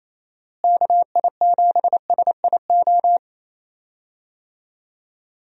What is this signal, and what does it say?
Morse code, heard as a keyed tone, about 28 wpm, 700 Hz. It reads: KI7SIO
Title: USA Call Signs - 6 Characters - Repeated 2x - 28wpm